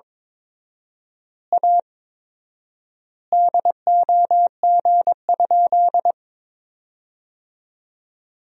Morse code, heard as a keyed tone, about 22 wpm, 700 Hz. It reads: A DOG?
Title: 4x Word-Spacing - Sentences from Top 400 Words - 22wpm